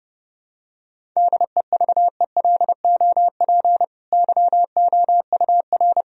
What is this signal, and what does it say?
Morse code, heard as a keyed tone, about 30 wpm, 700 Hz. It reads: DEVELOP YOUR
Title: Sentences from Top 1000 Words - Review Part 2 - 30wpm